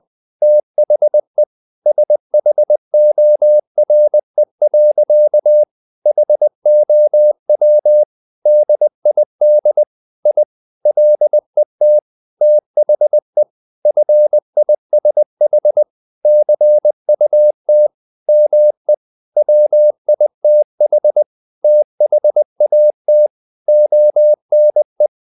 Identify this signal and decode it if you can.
Morse code, heard as a keyed tone, about 20 wpm, 600 Hz. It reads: THE SHORE. HOW DID I LET THE FISH CUT ME WITH THAT ONE
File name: the old man and the sea / the old man and the sea part 1 20wpm